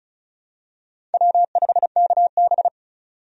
Morse code, heard as a keyed tone, about 36 wpm, 700 Hz. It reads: W5KB
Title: USA Call Signs - 4 Characters - 35wpm